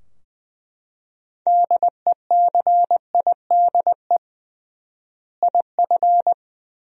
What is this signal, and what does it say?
Morse code, heard as a keyed tone, about 20 wpm, 700 Hz. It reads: DECIDE IF
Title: 3x Word-Spacing - Sentences from Top 500 Words - 20wpm